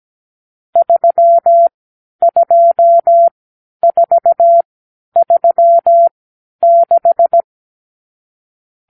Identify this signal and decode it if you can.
Morse code, heard as a keyed tone, about 17 wpm, spaced as for 10 wpm, 680 Hz. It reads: 32436
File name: k7qo / Lesson 50 (1-6)